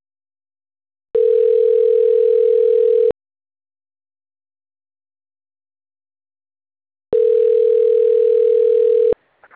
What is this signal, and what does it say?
Ringback tone (North America), 440 + 480 Hz.
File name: autotestagent / 10-25-18.136..callcentertransfer. .058a96c5-5842-4d59-9210